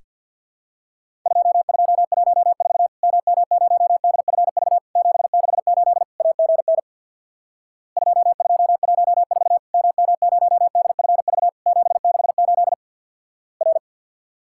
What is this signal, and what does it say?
Morse code, heard as a keyed tone, about 50 wpm, 700 Hz. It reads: 2214 MM0DFV 768 AGN 2214 MM0DFV 768 R
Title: Worked All Europe DX Contest - QTC Exchange with Repeats - 50wpm